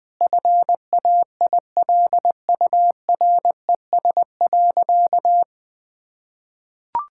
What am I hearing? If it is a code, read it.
Morse code, heard as a keyed tone, about 20 wpm, 700 Hz. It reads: FAILURES.
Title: Sentences with 7-Word Phrases - 20wpm